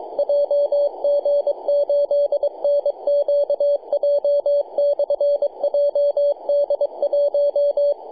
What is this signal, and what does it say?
Morse code, heard as a keyed tone, about 23 wpm, 570 Hz, with radio noise. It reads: JG8NQJ/JD1
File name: jg8nqj-jd1-21-076mhz-cw